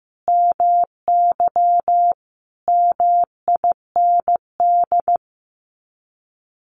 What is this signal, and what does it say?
Morse code, heard as a keyed tone, about 15 wpm, 700 Hz. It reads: MY MIND
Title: Sets of 5 Words Encore2 15wpm